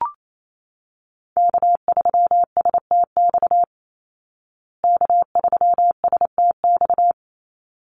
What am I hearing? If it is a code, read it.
Morse code, heard as a keyed tone, about 28 wpm, 700 Hz. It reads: K3STX K3STX
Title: Call Signs Repeated 3x - 28wpm